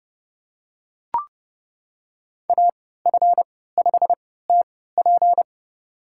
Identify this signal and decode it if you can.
Morse code, heard as a keyed tone, about 30 wpm, spaced as for 17 wpm, 700 Hz. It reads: AF5TP
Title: USA Call Signs - 5 Characters - 3x Character Spacing - 30wpm